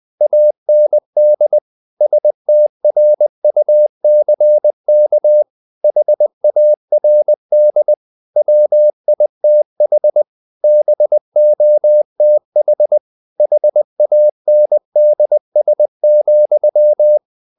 Morse code, heard as a keyed tone, about 20 wpm, 600 Hz. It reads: AND STRUCK HARD WITH BOTH HANDS,
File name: the old man and the sea / the old man and the sea part 1 20wpm